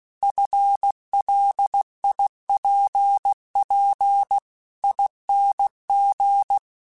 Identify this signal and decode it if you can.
Morse code, heard as a keyed tone, about 16 wpm, 800 Hz. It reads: FLIPP ING
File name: FL / FL mp3s / Fast FL @15